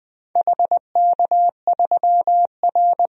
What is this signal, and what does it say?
Morse code, heard as a keyed tone, about 20 wpm, 700 Hz. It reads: HK3R
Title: DXCC-100 Call Signs - 4 Characters - 20wpm